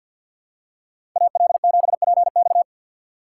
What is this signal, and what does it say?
Morse code, heard as a keyed tone, about 52 wpm, 690 Hz. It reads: AL7PX
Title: USA Call Signs - 5 Characters - 6x Character Spacing - 50wpm